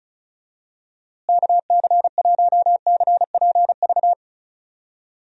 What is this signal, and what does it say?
Morse code, heard as a keyed tone, about 36 wpm, 700 Hz. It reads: KC1CPV
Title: USA Call Signs - 6 Characters - Repeated 3x - 35wpm